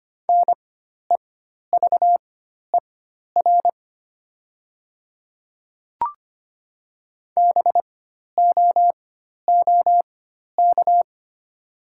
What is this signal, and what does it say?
Morse code, heard as a keyed tone, about 25 wpm, spaced as for 12 wpm, 700 Hz. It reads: NEVER BOOK
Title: 4x Character Spacing - Top 500 Words - 25wpm